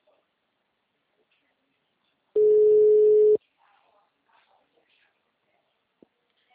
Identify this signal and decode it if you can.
Ringback tone (Europe, 425 Hz).